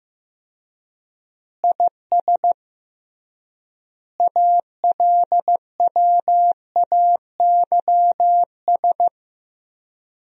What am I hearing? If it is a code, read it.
Morse code, heard as a keyed tone, about 15 wpm, 700 Hz. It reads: IS ALWAYS
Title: 3x Word-Spacing - Sentences from Top 400 Words - 15wpm